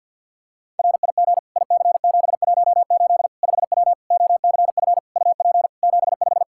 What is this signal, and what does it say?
Morse code, heard as a keyed tone, about 49 wpm, 700 Hz. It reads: RIG IC718 5W OCF UP 75
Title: K1USN SST QSO Practice 50wpm